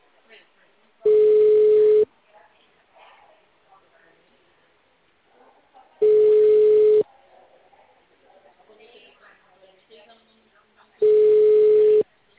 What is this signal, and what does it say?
Ringback tone (Europe, 425 Hz), with speech.